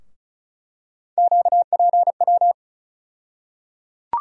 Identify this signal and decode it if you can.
Morse code, heard as a keyed tone, about 36 wpm, 700 Hz. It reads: QPW
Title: USA Call Sign Postfixes - 3 Letters 35wpm